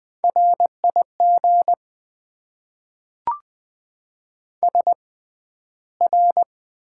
Morse code, heard as a keyed tone, about 20 wpm, spaced as for 12 wpm, 700 Hz. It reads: RIG SR
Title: Top 90 QSO Elements - 6x Character Spacing - 20wpm